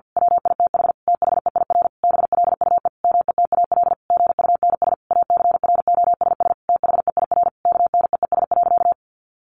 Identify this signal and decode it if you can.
Morse code, heard as a keyed tone, about 50 wpm, 700 Hz. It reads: WITH THEIR BLUE METAL GUNS ACROSS THEIR KNEES.
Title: A Sound Of Thunder 50wpm